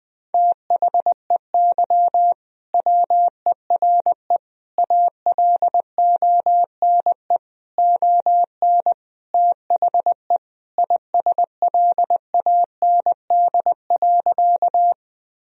Morse code, heard as a keyed tone, about 20 wpm, 700 Hz. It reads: THEY WERE ALONE ON THE ISLAND.
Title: Sentences from Top 1000 Words - Review Part 3 - 20wpm